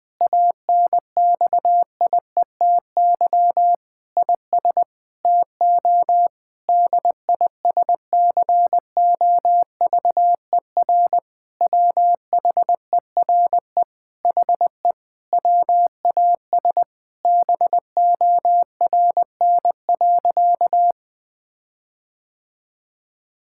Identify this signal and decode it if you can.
Morse code, heard as a keyed tone, about 20 wpm, 700 Hz. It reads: ANXIETY IS TO DISCOVER WHERE HE WAS BORN.
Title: Sentences from Top 1600 Words 20wpm